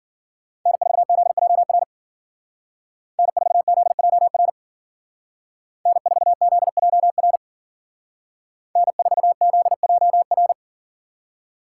Morse code, heard as a keyed tone, about 52 wpm, 700 Hz. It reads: N4ZJR N4ZJR N4ZJR N4ZJR